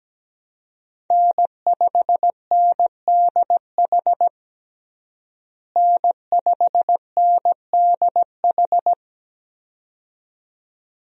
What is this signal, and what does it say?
Morse code, heard as a keyed tone, about 17 wpm, 700 Hz. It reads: N5NDH N5NDH
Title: USA Call Signs - 5 Characters - Repeated 2x - 17wpm